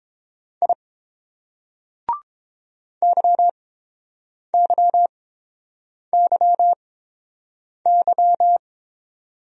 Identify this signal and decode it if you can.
Morse code, heard as a keyed tone, about 29 wpm, spaced as for 7 wpm, 700 Hz. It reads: I YYYY